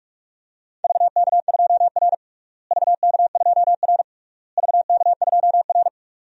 Morse code, heard as a keyed tone, about 46 wpm, 700 Hz. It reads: VK2R VK2R VK2R
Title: DXCC-100 Call Signs - 4 Characters - Repeated 3x - 45wpm